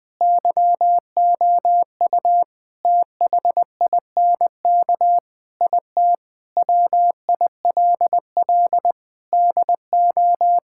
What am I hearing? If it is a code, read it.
Morse code, heard as a keyed tone, about 20 wpm, 700 Hz. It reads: YOU THINK IT WILL DO
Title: Sentences from Top 1000 Words - Review Part 6 - 20wpm